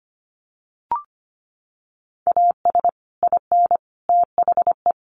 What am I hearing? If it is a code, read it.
Morse code, heard as a keyed tone, about 25 wpm, 700 Hz. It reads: AS IN THE
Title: Sets of 5 Words 25wpm